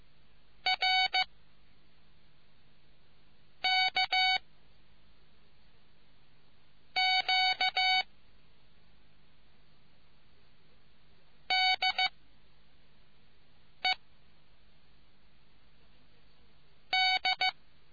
Morse code, heard as a keyed tone, about 15 wpm, 750 Hz. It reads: R K Q D E D